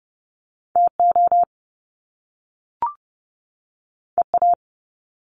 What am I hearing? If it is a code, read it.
Morse code, heard as a keyed tone, about 30 wpm, 700 Hz. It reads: TO EA
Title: Top Two Letter Combinations in Words 30wpm